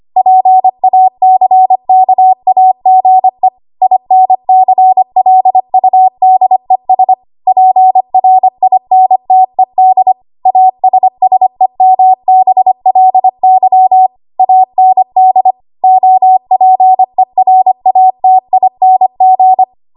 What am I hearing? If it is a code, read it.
Morse code, heard as a keyed tone, about 25 wpm, 750 Hz. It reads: PACKAGE INCLUDES PRINTED ASSEMBLY AND OPERATING